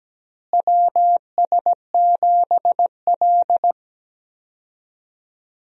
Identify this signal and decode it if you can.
Morse code, heard as a keyed tone, about 17 wpm, 700 Hz. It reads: WS7L
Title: USA Call Signs - 4 Characters - Repeated 2x - 17wpm